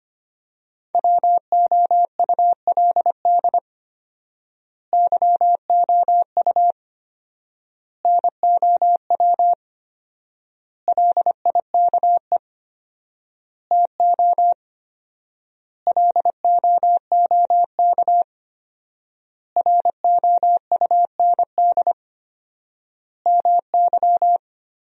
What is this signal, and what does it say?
Morse code, heard as a keyed tone, about 25 wpm, 700 Hz. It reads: WOULD YOU NOW LIKE TO LOOK ROUND MY
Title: 4x Word-Spacing - Sentences from Top 200 Words - 25wpm